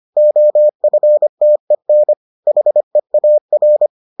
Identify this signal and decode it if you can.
Morse code, heard as a keyed tone, about 25 wpm, 600 Hz. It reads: OFTEN HEAR